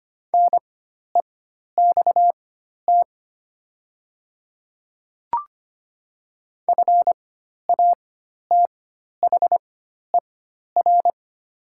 Morse code, heard as a keyed tone, about 25 wpm, spaced as for 12 wpm, 700 Hz. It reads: NEXT FATHER
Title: 4x Character Spacing - Top 300 Words - 25wpm